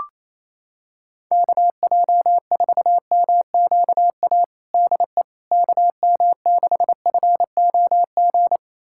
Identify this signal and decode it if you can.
Morse code, heard as a keyed tone, about 28 wpm, 700 Hz. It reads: KJ4MQA DE KM6FOG
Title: USA Callsign DE Callsign - 6 Characters - 28wpm